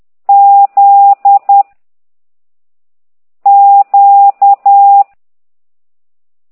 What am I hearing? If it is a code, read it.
Morse code, heard as a keyed tone, about 10 wpm, 800 Hz. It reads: Z Q